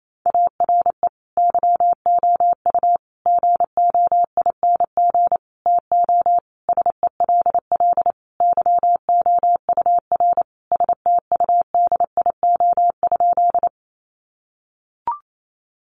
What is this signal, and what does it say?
Morse code, heard as a keyed tone, about 28 wpm, 700 Hz. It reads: ARE YOU GOING TO SELL YOUR STUDIO?
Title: Sentences from Top 1200 Words 28wpm